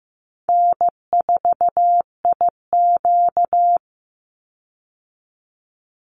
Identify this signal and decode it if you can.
Morse code, heard as a keyed tone, about 15 wpm, 700 Hz. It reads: N4IQ
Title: Easy Call Signs Repeated 3x - 15wpm